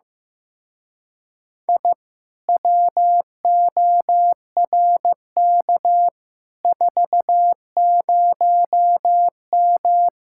Morse code, heard as a keyed tone, about 15 wpm, 700 Hz. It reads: I WORK 40M
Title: Speed-Racing Common QSO Phrases Encore - 15wpm